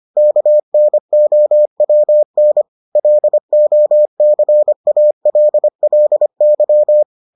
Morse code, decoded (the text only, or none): KNOWN LOCALLY